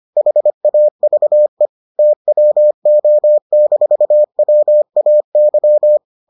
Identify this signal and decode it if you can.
Morse code, heard as a keyed tone, about 25 wpm, 600 Hz. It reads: HAVE TWO-WAY